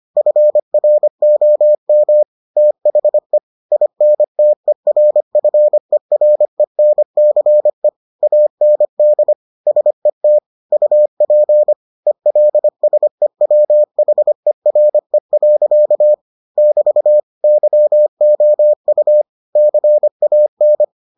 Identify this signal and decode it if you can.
Morse code, heard as a keyed tone, about 25 wpm, 600 Hz. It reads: FROM THE INTERFERENCE AND SET UP ELSEWHERE. <BT> YOU CAN